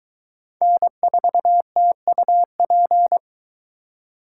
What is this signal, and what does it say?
Morse code, heard as a keyed tone, about 23 wpm, 700 Hz. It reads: N4TUP